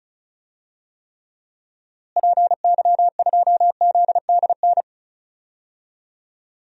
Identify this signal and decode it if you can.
Morse code, heard as a keyed tone, about 36 wpm, 700 Hz. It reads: PY2ZDN
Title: DXCC-100 Call Signs - 6 Characters - Repeated 3x - 35wpm